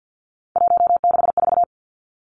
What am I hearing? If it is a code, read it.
Morse code, heard as a keyed tone, about 50 wpm, 700 Hz. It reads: 164
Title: Three Challenging Letter-Number Combinations 50wpm